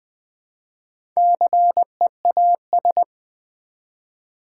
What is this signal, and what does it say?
Morse code, heard as a keyed tone, about 20 wpm, 700 Hz. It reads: CEAS
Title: Build A Word - 1 Letter At A Time - 5-Letter Words - 20wpm